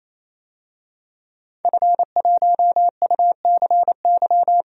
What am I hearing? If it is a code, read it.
Morse code, heard as a keyed tone, about 28 wpm, 700 Hz. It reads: F1UCY
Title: DXCC-100 Call Signs - 5 Characters - Repeated 3x - 28wpm